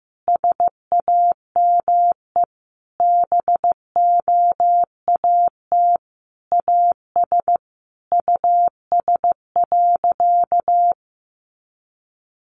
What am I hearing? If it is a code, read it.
Morse code, heard as a keyed tone, about 15 wpm, 700 Hz. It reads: SAME BOAT AS US.
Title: Sentences from Top 500 Words 15wpm